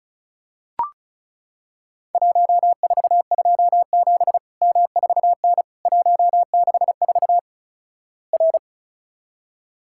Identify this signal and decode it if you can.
Morse code, heard as a keyed tone, about 36 wpm, 700 Hz. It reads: 1427 M4N 164 R
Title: Worked All Europe DX Contest - QTC Exchange - 35wpm